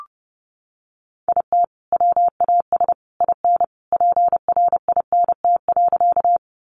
Morse code, heard as a keyed tone, about 28 wpm, 700 Hz. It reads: IT WAS IN PRINT.
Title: Sentences from Top 1400 Words 30wpm